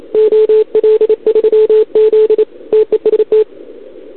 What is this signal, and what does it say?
Morse code, heard as a keyed tone, about 29 wpm, 440 Hz. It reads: OL3Z TEST
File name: CQWWCW2021 / OL3Z